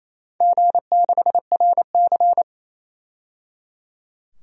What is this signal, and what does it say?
Morse code, heard as a keyed tone, about 28 wpm, 700 Hz. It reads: G6RC
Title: DXCC-100 Call Signs - 4 Characters - 28wpm